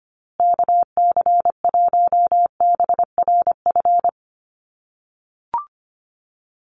Morse code, heard as a keyed tone, about 25 wpm, 700 Hz. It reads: KC1BRF
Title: USA Call Signs Repeated 2x 25wpm